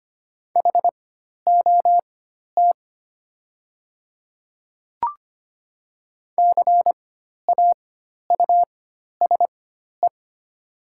Morse code, heard as a keyed tone, about 25 wpm, spaced as for 12 wpm, 700 Hz. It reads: HOT CAUSE
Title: 4x Character Spacing - Top 200 Words - 25wpm